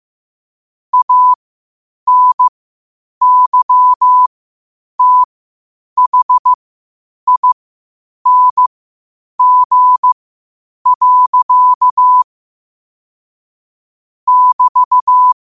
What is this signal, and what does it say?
Morse code, heard as a keyed tone, about 15 wpm, spaced as for 8 wpm, 1000 Hz. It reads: ANYTHING. <BT>
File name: qotdfiles / 8wpm